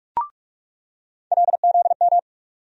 Morse code, heard as a keyed tone, about 45 wpm, 700 Hz. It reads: LZM
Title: USA Call Sign Postfixes - 3 Letters 45wpm